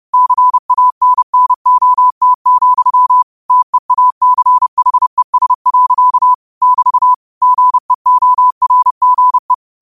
Morse code, heard as a keyed tone, about 30 wpm, 1000 Hz. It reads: CANNOT, TEACHES. <BT> GEORGE
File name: qotdfiles / 30wpm